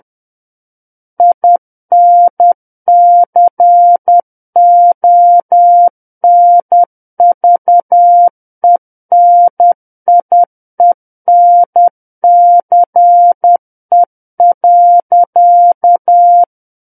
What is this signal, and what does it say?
Morse code, heard as a keyed tone, about 10 wpm, 700 Hz. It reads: INCONVENIENCE.
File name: Book One / 01-Chapter 1 - 10 WPM